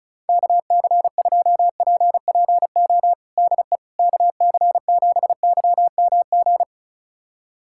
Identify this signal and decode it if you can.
Morse code, heard as a keyed tone, about 34 wpm, 700 Hz. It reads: KC2PPO DE KC7YMG
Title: USA Callsign DE Callsign - 6 Characters - 35wpm